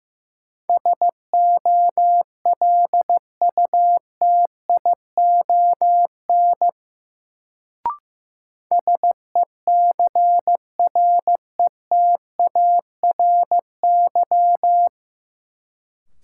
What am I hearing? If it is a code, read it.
Morse code, heard as a keyed tone, about 15 wpm, 700 Hz. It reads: SOLUTION SECRETARY
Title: Sentences from Top 1300 Words - New Word Drill Alt - 15wpm